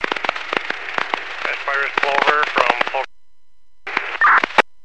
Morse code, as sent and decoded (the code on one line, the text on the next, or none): none